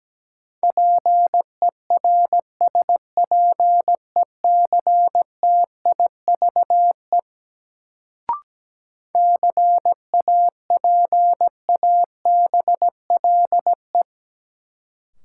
Morse code, heard as a keyed tone, about 17 wpm, 700 Hz. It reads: PERSPECTIVE CAPABLE
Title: Sentences from Top 1200 Words - New Word Drill - 17wpm